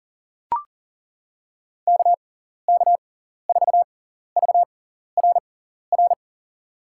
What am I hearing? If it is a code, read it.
Morse code, heard as a keyed tone, about 40 wpm, spaced as for 14 wpm, 700 Hz. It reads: KK4VRR
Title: USA Call Signs - 6 Characters - 6x Character Spacing - 40wpm